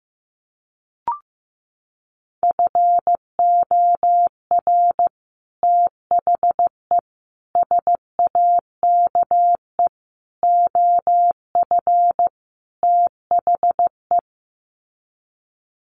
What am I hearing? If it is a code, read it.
Morse code, heard as a keyed tone, about 15 wpm, 700 Hz. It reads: FOR THE SAKE OF THE
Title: Sets of 5 Words Encore 15wpm